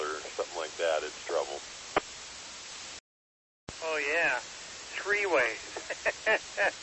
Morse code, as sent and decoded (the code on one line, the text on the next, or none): none